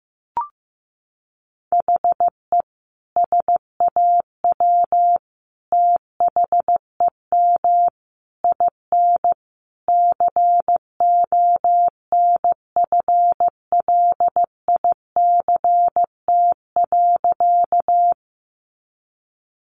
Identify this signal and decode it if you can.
Morse code, heard as a keyed tone, about 15 wpm, 700 Hz. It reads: HE SAW THEM IN CONFLICT.
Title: Sentences from Top 1300 Words 15wpm